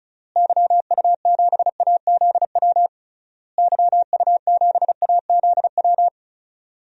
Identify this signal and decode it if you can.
Morse code, heard as a keyed tone, about 36 wpm, 700 Hz. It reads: YU7AZW YU7AZW
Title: DXCC-100 Call Signs - 6 Characters - Repeated 3x - 35wpm